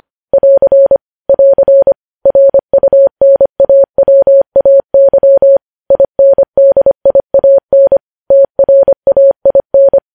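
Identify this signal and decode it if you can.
Morse code, heard as a keyed tone, about 25 wpm, 550 Hz. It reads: <AR> <AR> RUNAWAY INDIAN TRAIN
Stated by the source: News headlines sent at 25.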